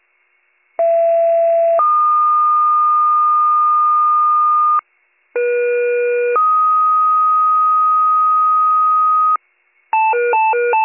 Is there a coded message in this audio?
Yes: two-tone page from 0.8 to 9.4 s, heard as a one-second tone, then a three-second tone, about 670 then 1220 Hz, about 480 then 1250 Hz.